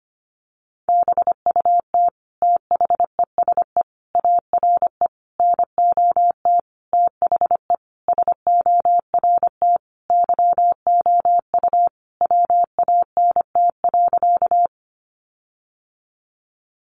Morse code, heard as a keyed tone, about 25 wpm, 700 Hz. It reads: BUT THESE ARE NOT THE SORT YOU WANT.
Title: Sentences from Top 1300 Words 25wpm